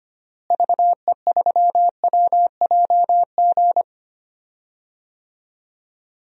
Morse code, heard as a keyed tone, about 25 wpm, 700 Hz. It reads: VE3WJG
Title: DXCC-100 Call Signs - 6 Characters - Repeated 2x - 25wpm